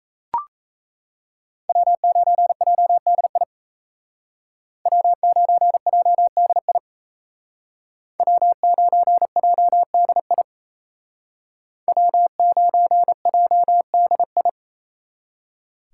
Morse code, heard as a keyed tone, about 33 wpm, 700 Hz. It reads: W9JDI W9JDI W9JDI W9JDI
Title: Speed Racing - USA 1x3 Call Signs 28wpm